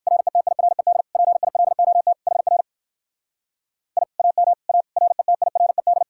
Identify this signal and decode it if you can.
Morse code, heard as a keyed tone, about 59 wpm, 700 Hz. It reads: RETIRED PILOT HR I AM A RETIRED